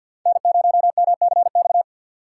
Morse code, heard as a keyed tone, about 53 wpm, 700 Hz. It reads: N0MCX
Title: USA Call Signs - 5 Characters - 3x Character Spacing - 50wpm